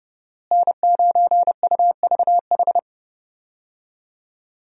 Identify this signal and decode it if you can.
Morse code, heard as a keyed tone, about 30 wpm, 700 Hz. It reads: N9UVH
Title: USA Call Signs - 5 Characters - Repeated 3x - 30wpm